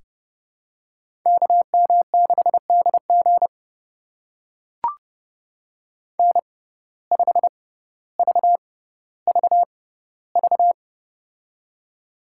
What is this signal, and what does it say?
Morse code, heard as a keyed tone, about 32 wpm, spaced as for 16 wpm, 700 Hz. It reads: KM6DG N5VVV
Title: USA Call Signs - 5 Characters - 6x Character Spacing - 30wpm